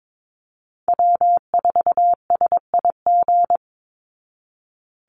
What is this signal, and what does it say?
Morse code, heard as a keyed tone, about 22 wpm, 700 Hz. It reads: W4SIG
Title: Call Signs Repeated 3x - 22wpm